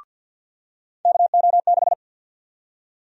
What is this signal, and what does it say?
Morse code, heard as a keyed tone, about 52 wpm, 700 Hz. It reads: KO6
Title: USA Call Sign Prefixes - 50wpm